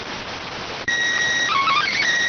SSTV (slow-scan television), PD 290.